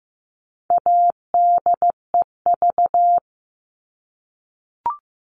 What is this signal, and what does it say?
Morse code, heard as a keyed tone, about 15 wpm, 700 Hz. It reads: ADEV
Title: All Cut Numbers - 4-Digit 15wpm